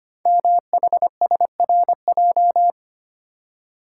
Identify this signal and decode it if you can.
Morse code, heard as a keyed tone, about 25 wpm, 700 Hz. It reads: MHSRJ
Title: Code Groups - 5 Letters - 25wpm